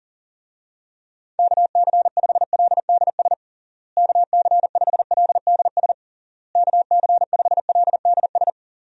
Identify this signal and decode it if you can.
Morse code, heard as a keyed tone, about 37 wpm, 700 Hz. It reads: KC5LDS KC5LDS KC5LDS
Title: USA Call Signs - 6 Characters - Repeated 3x - 40wpm